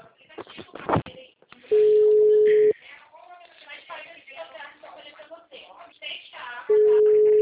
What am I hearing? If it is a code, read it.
Ringback tone (Europe, 425 Hz), with speech.